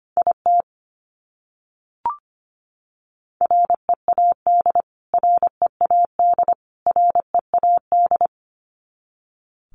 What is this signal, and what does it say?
Morse code, heard as a keyed tone, about 25 wpm, 700 Hz. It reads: IT READ READ READ
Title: Top 200 Words Repeated 25wpm